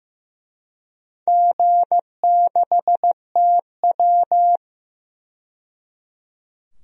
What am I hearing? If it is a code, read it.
Morse code, heard as a keyed tone, about 15 wpm, 700 Hz. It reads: G6TW
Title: DXCC-100 Call Signs - 4 Characters - 15wpm